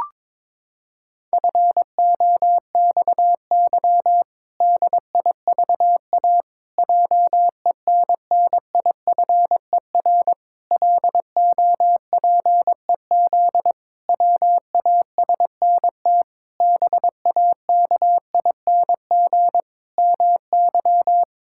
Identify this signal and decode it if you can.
Morse code, heard as a keyed tone, about 22 wpm, 700 Hz. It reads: FOXY DIVA JENNIFER LOPEZ WASNT BAKING MY
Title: Pangram Challenge v2 - 22wpm